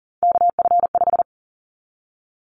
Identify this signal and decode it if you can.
Morse code, heard as a keyed tone, about 40 wpm, 700 Hz. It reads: KF5
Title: Callsign Prefixes for Top 25 DXCC - Repeated 3x - 40wpm